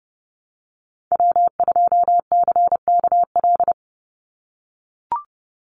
Morse code, heard as a keyed tone, about 30 wpm, 700 Hz. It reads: W2CKL
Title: NA SOTA Activator Callsigns 30wpm